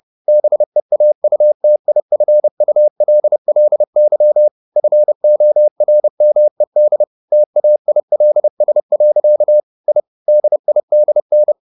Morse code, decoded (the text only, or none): BEAUTIFULLY FORMED TAILS. I DIDN